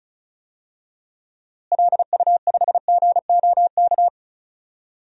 Morse code, heard as a keyed tone, about 36 wpm, 700 Hz. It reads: LU5GOK
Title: DXCC-100 Call Signs - 6 Characters - Repeated 3x - 35wpm